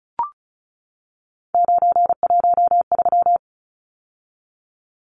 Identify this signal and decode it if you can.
Morse code, heard as a keyed tone, about 34 wpm, 700 Hz. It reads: 913
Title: Numbers - 3-Digit 35wpm